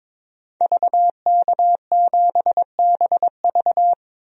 Morse code, heard as a keyed tone, about 22 wpm, 700 Hz. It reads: VK7BV
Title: DXCC-100 Call Signs - 5 Characters - Repeated 2x - 22wpm